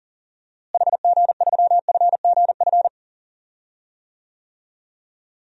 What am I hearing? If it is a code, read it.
Morse code, heard as a keyed tone, about 37 wpm, 700 Hz. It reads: HG3FGF
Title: DXCC-100 Call Signs - 6 Characters - Repeated 3x - 40wpm